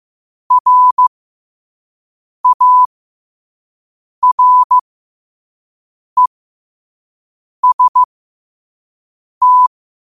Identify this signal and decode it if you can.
Morse code, heard as a keyed tone, about 15 wpm, spaced as for 5 wpm, 1000 Hz. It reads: RAREST
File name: qotdfiles / 5wpm